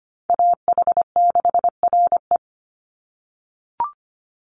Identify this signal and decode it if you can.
Morse code, heard as a keyed tone, about 25 wpm, 700 Hz. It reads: AH6RE
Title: Call Signs - 25wpm